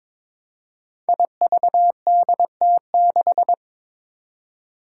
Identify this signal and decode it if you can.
Morse code, heard as a keyed tone, about 22 wpm, 700 Hz. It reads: IVDT6
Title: Five Challenging Letter-Number Combinations 22wpm